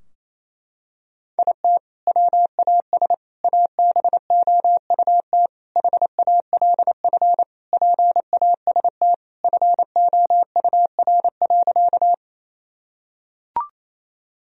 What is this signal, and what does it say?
Morse code, heard as a keyed tone, about 28 wpm, 700 Hz. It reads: IT WAS ABOUT HALF PAST FOUR.
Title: Sentences from Top 1000 Words - Review Part 3 - 28wpm